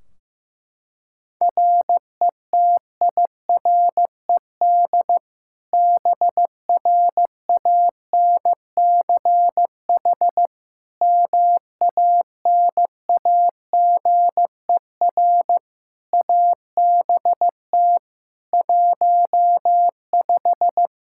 Morse code, heard as a keyed tone, about 15 wpm, 700 Hz. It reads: RETIRED BRANCH MANAGER ABT 15
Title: Practice QSO - Exchange 3 - More Info - 15wpm